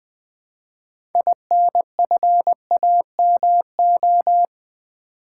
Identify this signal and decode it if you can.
Morse code, heard as a keyed tone, about 20 wpm, 700 Hz. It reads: INFAMO
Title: Build A Word - 2 Letters At A Time - 8-Letter Words - 20wpm